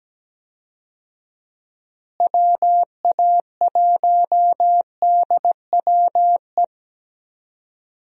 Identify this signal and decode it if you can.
Morse code, heard as a keyed tone, about 17 wpm, 700 Hz. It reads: WA1DWE
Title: USA Call Signs - 6 Characters - Repeated 3x - 17wpm